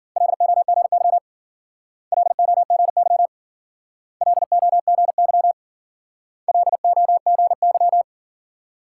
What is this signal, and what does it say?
Morse code, heard as a keyed tone, about 48 wpm, 700 Hz. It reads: LOGY LOGY LOGY LOGY
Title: Speed-Racing Multi-letter Phonemes - 40wpm